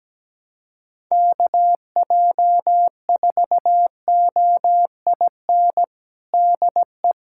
Morse code, heard as a keyed tone, about 17 wpm, 700 Hz. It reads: KJ4OIN DE
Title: USA Callsign DE Callsign 17wpm